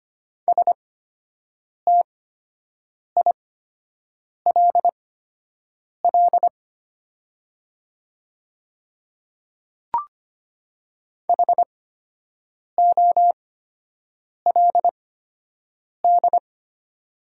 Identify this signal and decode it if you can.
Morse code, heard as a keyed tone, about 25 wpm, spaced as for 7 wpm, 700 Hz. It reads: STILL HOLD
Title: 8x Character Spacing - Top 500 Words - 25wpm